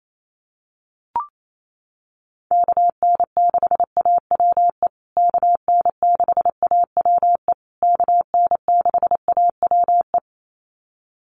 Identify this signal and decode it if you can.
Morse code, heard as a keyed tone, about 28 wpm, 700 Hz. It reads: KN6AWE KN6AWE KN6AWE
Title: USA Call Signs Repeated 3x 28wpm